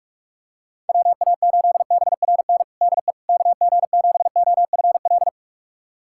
Morse code, heard as a keyed tone, about 46 wpm, 700 Hz. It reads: WA8BRN DE KG7OFL
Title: USA Callsign DE Callsign - 6 Characters - 45wpm